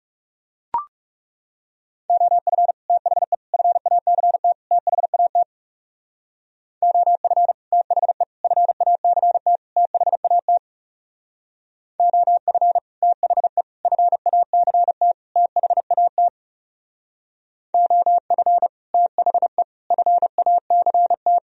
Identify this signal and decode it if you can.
Morse code, heard as a keyed tone, about 35 wpm, 700 Hz. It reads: OF THE FACT THAT OF THE FACT THAT OF THE FACT THAT OF THE FACT
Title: Speed-Racing - Sets of 4 Words 30wpm